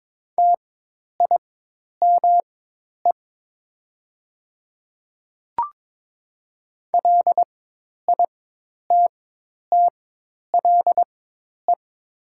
Morse code, heard as a keyed tone, about 22 wpm, spaced as for 10 wpm, 700 Hz. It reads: TIME LITTLE
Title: 4x Character Spacing - Top 200 Words - 22wpm